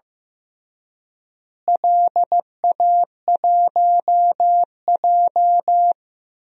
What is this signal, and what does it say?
Morse code, heard as a keyed tone, about 15 wpm, 700 Hz. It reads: LA1J
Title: DXCC-100 Call Signs - 4 Characters - Repeated 3x - 15wpm